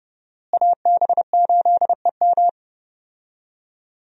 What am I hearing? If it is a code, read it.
Morse code, heard as a keyed tone, about 31 wpm, 700 Hz. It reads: AB8EM
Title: USA Call Signs - 5 Characters - Repeated 2x - 30wpm